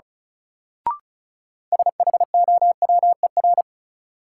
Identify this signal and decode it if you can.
Morse code, heard as a keyed tone, about 36 wpm, 700 Hz. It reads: SHOWER
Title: Sentences from Top 1600 Words - New Word Drill - 35wpm